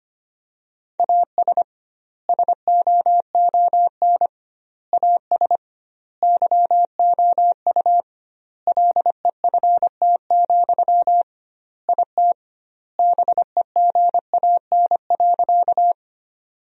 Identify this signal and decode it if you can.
Morse code, heard as a keyed tone, about 25 wpm, 700 Hz. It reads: AS SOON AS YOU LEFT, IT BEGAN.
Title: 2x Word-Spacing - Sentences from Top 400 Words - 25wpm